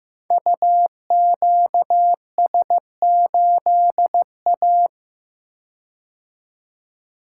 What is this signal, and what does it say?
Morse code, heard as a keyed tone, about 15 wpm, 700 Hz. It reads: UQS8A
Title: Code Groups - 5 Letter:Numbers - 15wpm